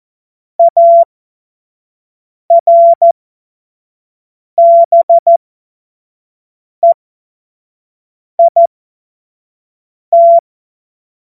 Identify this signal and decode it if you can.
Morse code, heard as a keyed tone, about 14 wpm, spaced as for 5 wpm, 680 Hz. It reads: ARBEIT
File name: Morsekurs / Witze / Witz 29. Juni 2021, 25 F70